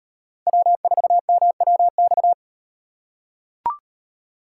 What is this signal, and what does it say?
Morse code, heard as a keyed tone, about 37 wpm, 700 Hz. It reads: W4MWX